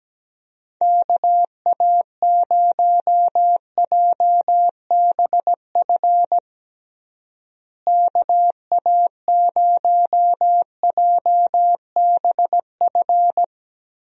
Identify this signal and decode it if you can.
Morse code, heard as a keyed tone, about 17 wpm, 700 Hz. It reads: KA0JBF KA0JBF
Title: USA Call Signs - 6 Characters - Repeated 3x - 17wpm